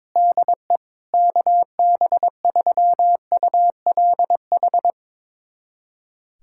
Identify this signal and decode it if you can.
Morse code, heard as a keyed tone, about 22 wpm, 700 Hz. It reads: DE KB3ULH
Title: USA Callsign DE Callsign - 6 Characters - 22wpm